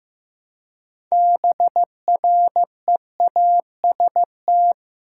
Morse code, heard as a keyed tone, about 15 wpm, 700 Hz. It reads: BREAST
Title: Six Letter Words - 15wpm